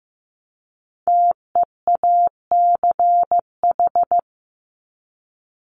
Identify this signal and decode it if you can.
Morse code, heard as a keyed tone, about 15 wpm, 700 Hz. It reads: TEACH
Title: Top 500 Words Repeated 15wpm